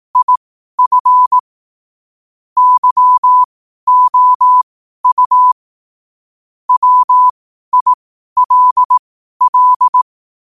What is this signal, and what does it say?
Morse code, heard as a keyed tone, about 18 wpm, spaced as for 12 wpm, 1000 Hz. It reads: IF YOU WILL
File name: qotdfiles / 12wpm